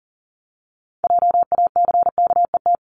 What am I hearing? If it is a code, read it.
Morse code, heard as a keyed tone, about 38 wpm, 700 Hz. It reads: JACKET
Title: Multi-letter Phonemes 40wpm